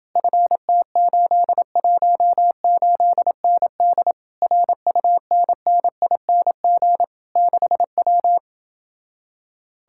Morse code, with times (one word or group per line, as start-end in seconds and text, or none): 0.2-4.1 FT818ND
4.4-7.0 RUNNING
7.4-8.4 6W